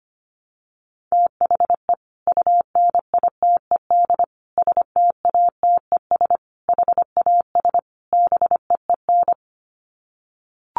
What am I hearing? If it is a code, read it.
Morse code, heard as a keyed tone, about 25 wpm, 700 Hz. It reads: THE UNITED STATES HAS BEEN
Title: Sets of 5 Words Encore2 25wpm